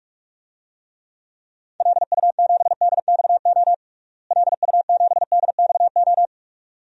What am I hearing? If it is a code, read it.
Morse code, heard as a keyed tone, about 46 wpm, 700 Hz. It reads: LU7DXO LU7DXO
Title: DXCC-100 Call Signs - 6 Characters - Repeated 2x - 45wpm